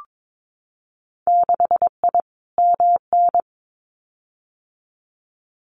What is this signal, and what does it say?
Morse code, heard as a keyed tone, about 22 wpm, 700 Hz. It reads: 6I MN
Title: Winter Field Day Exchanges Encore 22wpm